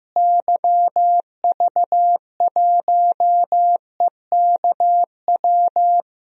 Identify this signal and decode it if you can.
Morse code, heard as a keyed tone, about 15 wpm, 700 Hz. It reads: YV1EKW
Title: DXCC-100 Call Signs - 6 Characters - Repeated 3x - 15wpm